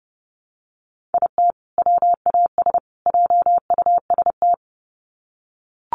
Morse code, heard as a keyed tone, about 30 wpm, 700 Hz. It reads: IT WAS JUST
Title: Sets of 3 Words 30wpm